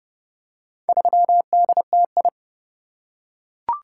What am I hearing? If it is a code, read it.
Morse code, heard as a keyed tone, about 27 wpm, 700 Hz. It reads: 3DTI